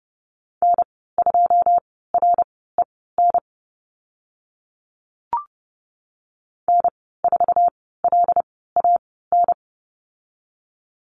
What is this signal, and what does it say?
Morse code, heard as a keyed tone, about 30 wpm, spaced as for 17 wpm, 700 Hz. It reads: N2REN N4LAN
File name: course / 30:10wpm / 088 Lesson 15g - Number 5 - Call Signs 30wpm